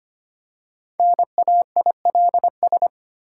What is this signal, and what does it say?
Morse code, heard as a keyed tone, about 25 wpm, 700 Hz. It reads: NAILS
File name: practice / 25wpm / Difficult Words with Hints - 25wpm